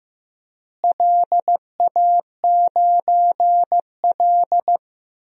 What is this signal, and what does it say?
Morse code, heard as a keyed tone, about 15 wpm, 700 Hz. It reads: LA9L
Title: DXCC-100 Call Signs - 4 Characters - Repeated 3x - 15wpm